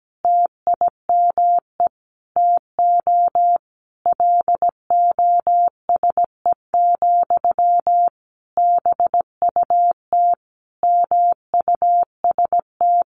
Morse code, heard as a keyed tone, about 17 wpm, 700 Hz. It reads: TIME TO LOSE, BUT MUST
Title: Sentences from Top 1100 Words 17wpm